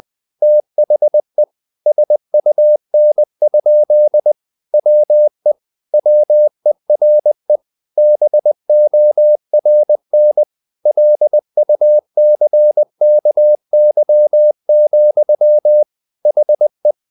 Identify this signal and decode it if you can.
Morse code, heard as a keyed tone, about 20 wpm, 600 Hz. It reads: THE SUN? WE WERE BORN LUCKY, HE
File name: the old man and the sea part 2 20wpm